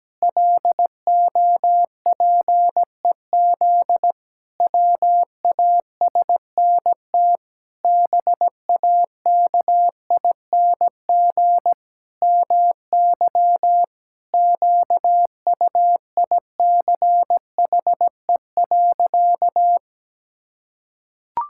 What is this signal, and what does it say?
Morse code, heard as a keyed tone, about 17 wpm, 700 Hz. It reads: LOPEZ WASNT BAKING MY QUICHE.
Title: Pangram Challenge v2 - 17wpm